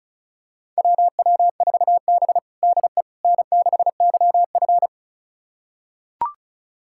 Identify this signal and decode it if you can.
Morse code, heard as a keyed tone, about 36 wpm, 700 Hz. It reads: WW4B DE N6YF
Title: USA Callsign DE Callsign - 4 Characters - 35wpm